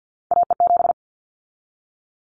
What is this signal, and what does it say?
Morse code, heard as a keyed tone, about 50 wpm, 700 Hz. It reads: AE7
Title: Callsign Prefixes for Top 25 DXCC - 50wpm